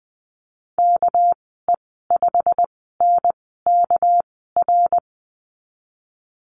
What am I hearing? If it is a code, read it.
Morse code, heard as a keyed tone, about 20 wpm, spaced as for 15 wpm, 700 Hz. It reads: KE5NKR
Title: 207 Lesson 33g - Letter j - Call Signs 20wpm